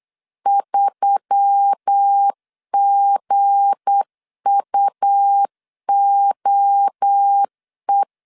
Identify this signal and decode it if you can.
Morse code, heard as a keyed tone, about 8 wpm, 800 Hz. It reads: 3GUOE
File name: LEZIONI / 9 2867 8wpm 3440